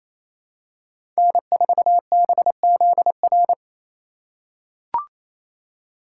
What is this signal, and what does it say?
Morse code, heard as a keyed tone, about 28 wpm, 700 Hz. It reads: N4BZR
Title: USA Call Signs - 3x Character Spacing - 28wpm